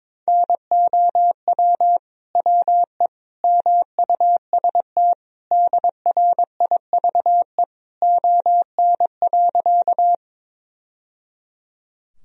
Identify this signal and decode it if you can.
Morse code, heard as a keyed tone, about 22 wpm, 700 Hz. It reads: NOW WE MUST DRIVE ON.
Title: Sentences from Top 1000 Words - Review Part 2 - 22wpm